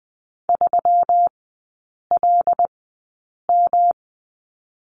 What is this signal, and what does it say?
Morse code, heard as a keyed tone, about 19 wpm, spaced as for 8 wpm, 700 Hz. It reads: 3LM
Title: 184 Lesson 30e - Letter b - Three Characters 20wpm